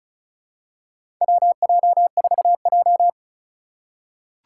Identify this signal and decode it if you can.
Morse code, heard as a keyed tone, about 36 wpm, 690 Hz. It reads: WJ4J
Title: USA Call Signs - 4 Characters - 3x Character Spacing - 35wpm